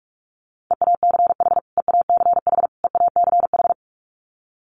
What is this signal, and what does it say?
Morse code, heard as a keyed tone, about 44 wpm, 700 Hz. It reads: EACH EACH EACH
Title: Top 200 Words Repeated 45wpm